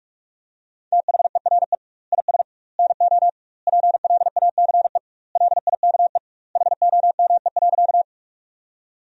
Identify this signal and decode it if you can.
Morse code, heard as a keyed tone, about 46 wpm, 700 Hz. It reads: THERE IS NO PLACE LIKE HOME.
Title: 2x Word-Spacing - Sentences from Top 200 Words - 45wpm